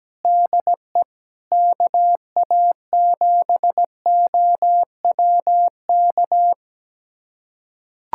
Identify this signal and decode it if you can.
Morse code, heard as a keyed tone, about 17 wpm, 700 Hz. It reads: DE KA7OWK
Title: USA Callsign DE Callsign 17wpm